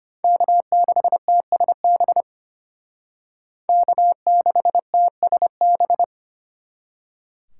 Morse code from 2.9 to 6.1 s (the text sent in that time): K6TSB